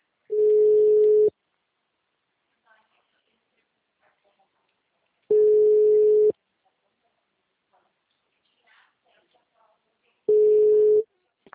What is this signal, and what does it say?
Ringback tone (Europe, 425 Hz), with speech.